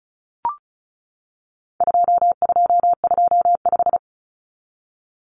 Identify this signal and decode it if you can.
Morse code, heard as a keyed tone, about 34 wpm, 700 Hz. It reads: 2225
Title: Common Cut Numbers - 4-Digit 35wpm